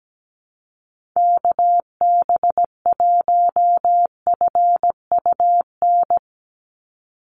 Morse code, heard as a keyed tone, about 17 wpm, 700 Hz. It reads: KB1FUN
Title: USA Call Signs 17wpm